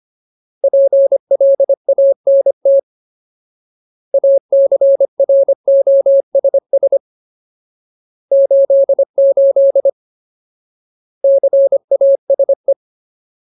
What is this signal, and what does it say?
Morse code, heard as a keyed tone, about 25 wpm, 550 Hz. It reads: PLANT ACROSS 88 CASE
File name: Ultimate Word List / 25WPM Ultimate